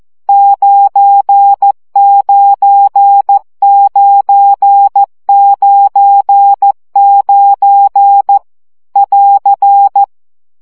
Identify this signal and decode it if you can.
Morse code, heard as a keyed tone, about 14 wpm, 800 Hz. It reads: 99999 <AR>
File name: CW gyakorlas 9 12 baud